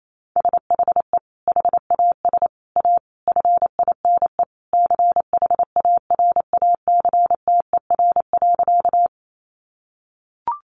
Morse code, heard as a keyed tone, about 28 wpm, 700 Hz. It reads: SHE HAS A FINE CHARACTER.
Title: Sentences from Top 1200 Words 28wpm